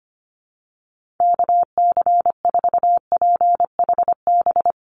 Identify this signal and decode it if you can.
Morse code, heard as a keyed tone, about 25 wpm, 700 Hz. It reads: KC4PHB
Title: USA Call Signs Repeated 2x 25wpm